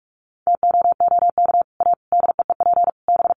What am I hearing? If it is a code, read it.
Morse code, heard as a keyed tone, about 44 wpm, 700 Hz. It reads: TOOK A DEEP B
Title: Sets of 5 Words Encore 45wpm